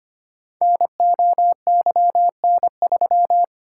Morse code, heard as a keyed tone, about 25 wpm, 700 Hz. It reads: NOYN3
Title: Code Groups - 5 Letter:Numbers - 25wpm